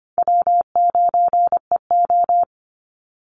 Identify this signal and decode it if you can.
Morse code, heard as a keyed tone, about 25 wpm, 700 Hz. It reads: W9EO
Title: Call Signs Repeated 3x - 25wpm